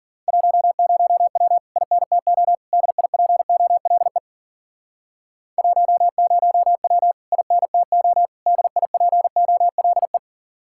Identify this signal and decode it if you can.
Morse code, heard as a keyed tone, about 50 wpm, 700 Hz. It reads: 10W INTO DIPOLE 10W INTO DIPOLE
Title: Speed-Racing Common QSO Phrases Encore - 40wpm